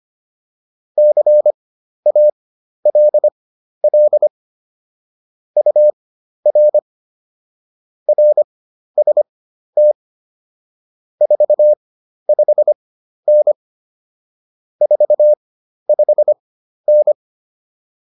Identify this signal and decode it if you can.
Morse code, heard as a keyed tone, about 25 wpm, spaced as for 12 wpm, 600 Hz. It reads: CALL UR RST 45N 45N